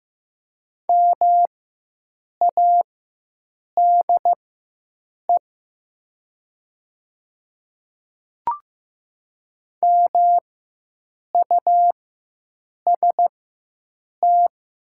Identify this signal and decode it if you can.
Morse code, heard as a keyed tone, about 15 wpm, spaced as for 7 wpm, 700 Hz. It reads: MADE MUST
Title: 4x Character Spacing - Top 300 Words - 15wpm